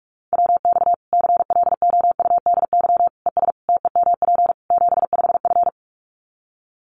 Morse code, heard as a keyed tone, about 44 wpm, 700 Hz. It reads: WX CLOUDY ES TEMP 75F
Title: Common QSO Phrases 45wpm